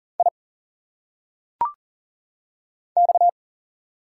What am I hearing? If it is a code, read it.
Morse code, heard as a keyed tone, about 40 wpm, 700 Hz. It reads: I X
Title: Single Letters - Phonetic Pronunciation 40wpm